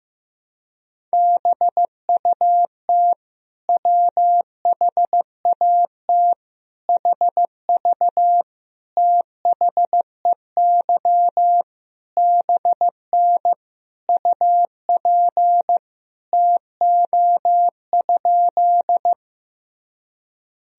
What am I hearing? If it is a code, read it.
Morse code, heard as a keyed tone, about 16 wpm, 700 Hz. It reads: BUT WHAT HV THEY BN UP TO?
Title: Sentences from Top 100 Words and Abbreviations 15wpm